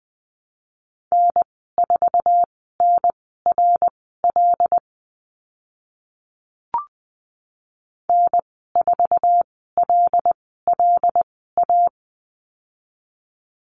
Morse code, heard as a keyed tone, about 20 wpm, spaced as for 15 wpm, 700 Hz. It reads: N4NRL N4LLA
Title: 076 Lesson 13g - Letter L - Call Signs 20wpm